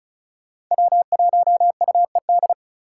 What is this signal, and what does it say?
Morse code, heard as a keyed tone, about 43 wpm, 700 Hz. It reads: W1UED
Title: Speed Racing - USA 1x3 Call Signs 35wpm